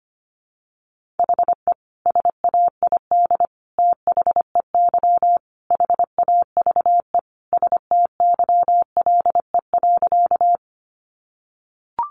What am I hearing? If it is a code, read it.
Morse code, heard as a keyed tone, about 25 wpm, 700 Hz. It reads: HE SAID THEY HAVE STYLE.
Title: Sentences from Top 800 Words 25wpm